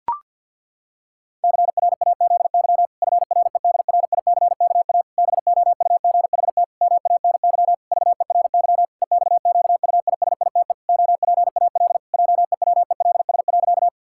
Morse code, heard as a keyed tone, about 52 wpm, 700 Hz. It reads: CRAZY FREDRICKA BOUGHT MANY VERY EXQUISITE OPAL JEWELS.
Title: Pangram Challenge v2 - 50wpm